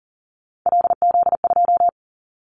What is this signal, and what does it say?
Morse code, heard as a keyed tone, about 38 wpm, 700 Hz. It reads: LZ2
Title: Callsign Prefixes for Top 50 DXCC - 40wpm